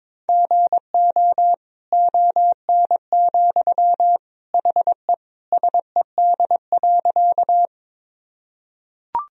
Morse code, heard as a keyed tone, about 22 wpm, 700 Hz. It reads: GO ON, HE SED.
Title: Sentences from Top 100 Words and Abbreviations 22wpm